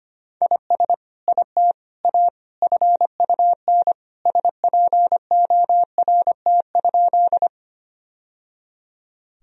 Morse code, heard as a keyed tone, about 25 wpm, 700 Hz. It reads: IS IT A FUN SPORT?
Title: Sentences from Top 1000 Words - Review Part 4 - 25wpm